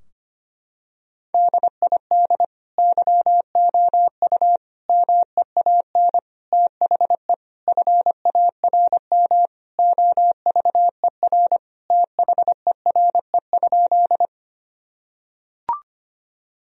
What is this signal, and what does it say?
Morse code, heard as a keyed tone, about 25 wpm, 700 Hz. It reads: DID YOU MEAN THE FARM OVER THERE?
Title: Sentences from Top 1000 Words - Review Part 2 - 25wpm